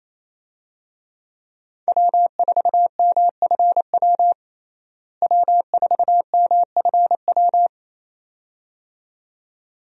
Morse code, heard as a keyed tone, about 28 wpm, 700 Hz. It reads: W4MFW W4MFW
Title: USA Call Signs - 5 Characters - Repeated 2x - 28wpm